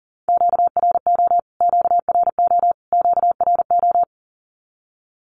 Morse code, heard as a keyed tone, about 40 wpm, 700 Hz. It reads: QRO QRO QRO
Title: Q-Codes Repeated 3x 40wpm